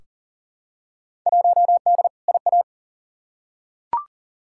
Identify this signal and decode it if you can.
Morse code, heard as a keyed tone, about 40 wpm, 700 Hz. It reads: 1D IA
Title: Field Day Exchanges - 40wpm